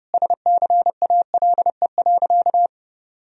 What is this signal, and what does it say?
Morse code, heard as a keyed tone, about 32 wpm, 700 Hz. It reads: SCALE.
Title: Sentences from Top 1000 Words - Review Part 4 - 30wpm